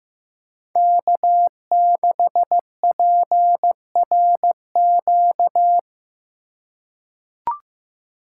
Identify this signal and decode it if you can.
Morse code, heard as a keyed tone, about 15 wpm, 700 Hz. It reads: K6PRQ
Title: USA Call Signs - 5 Characters - 2x Character Spacing - 15wpm